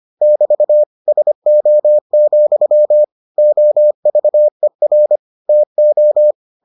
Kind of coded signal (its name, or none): Morse code